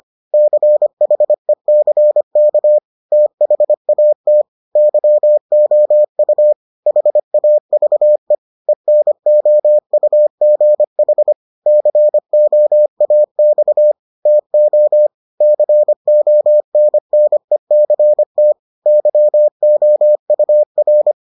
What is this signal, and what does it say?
Morse code, heard as a keyed tone, about 25 wpm, 600 Hz. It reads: CHECK THAT YOU HAVE ENOUGH COAX TO CONNECT YOUR